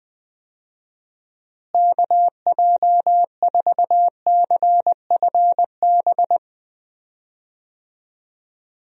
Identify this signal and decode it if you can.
Morse code, heard as a keyed tone, about 20 wpm, 700 Hz. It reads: KJ4CFB
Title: USA Call Signs - 6 Characters - Repeated 2x - 20wpm